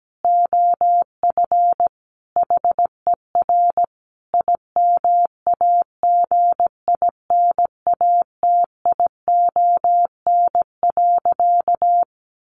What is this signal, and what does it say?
Morse code, heard as a keyed tone, about 17 wpm, 700 Hz. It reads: OF HER IMAGINATION.
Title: Sentences from Top 1400 Words 17wpm